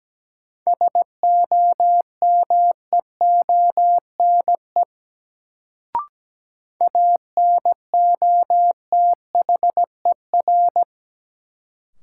Morse code, heard as a keyed tone, about 17 wpm, 700 Hz. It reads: SOMEONE ANOTHER
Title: Sentences from Top 600 Words - New Word Drill - 17wpm